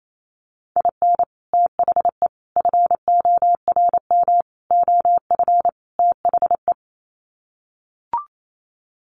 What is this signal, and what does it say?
Morse code, heard as a keyed tone, about 28 wpm, 700 Hz. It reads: IN THE FORM OF THE
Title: Sets of 5 Words Encore2 28wpm